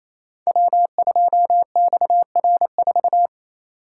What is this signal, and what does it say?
Morse code, heard as a keyed tone, about 27 wpm, 700 Hz. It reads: W2XR4
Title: Code Groups - 5 Letter:Numbers - 28wpm